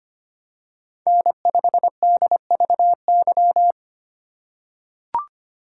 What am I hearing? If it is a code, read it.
Morse code, heard as a keyed tone, about 25 wpm, 700 Hz. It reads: N5DVY
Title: USA Call Signs - 4x Character Spacing - 25wpm